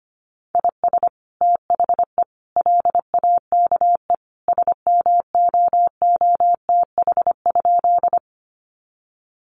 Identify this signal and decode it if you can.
Morse code, heard as a keyed tone, about 25 wpm, 700 Hz. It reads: IS THE LAKE SMOOTH?
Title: Sentences from Top 1400 Words 25wpm